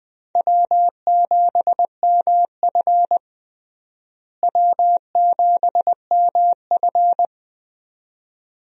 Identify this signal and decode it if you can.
Morse code, heard as a keyed tone, about 21 wpm, 700 Hz. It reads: W7MF W7MF
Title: USA Call Signs - 4 Characters - Repeated 2x - 20wpm